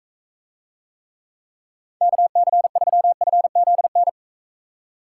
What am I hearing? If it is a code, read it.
Morse code, heard as a keyed tone, about 41 wpm, 700 Hz. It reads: KC3FZN